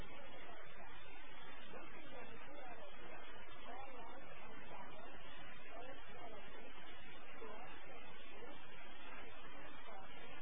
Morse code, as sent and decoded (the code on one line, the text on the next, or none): none